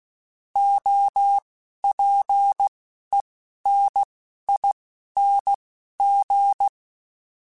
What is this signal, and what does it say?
Morse code, heard as a keyed tone, about 16 wpm, spaced as for 11 wpm, 800 Hz. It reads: OPENING
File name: Alphabetical / Alphabetical mp3s / Os @10